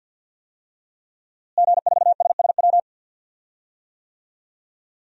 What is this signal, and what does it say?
Morse code, heard as a keyed tone, about 48 wpm, 700 Hz. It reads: G4SSW
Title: DXCC-100 Call Signs - 5 Characters - Repeated 3x - 50wpm